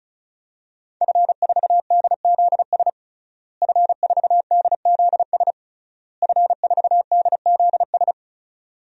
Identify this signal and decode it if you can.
Morse code, heard as a keyed tone, about 34 wpm, 700 Hz. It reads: F4DZS F4DZS F4DZS
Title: DXCC-100 Call Signs - 5 Characters - Repeated 3x - 35wpm